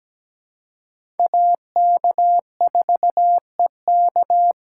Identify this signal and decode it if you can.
Morse code, heard as a keyed tone, about 17 wpm, 700 Hz. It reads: AK4EK
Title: USA Call Signs - 5 Characters - 2x Character Spacing - 17wpm